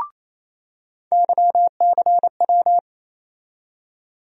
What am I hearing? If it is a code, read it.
Morse code, heard as a keyed tone, about 28 wpm, 700 Hz. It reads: YCW
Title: USA Call Sign Postfixes - 3 Letters 28wpm